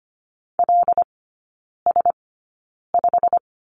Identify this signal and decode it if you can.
Morse code, heard as a keyed tone, about 25 wpm, spaced as for 9 wpm, 700 Hz. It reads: LS5
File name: Lesson 15e - Number 5 - Three Characters 25wpm